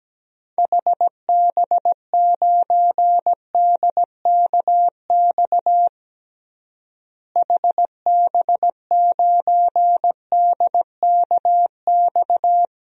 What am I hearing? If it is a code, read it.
Morse code, heard as a keyed tone, about 17 wpm, 700 Hz. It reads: HB9DKX HB9DKX
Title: DXCC-100 Call Signs - 6 Characters - Repeated 3x - 17wpm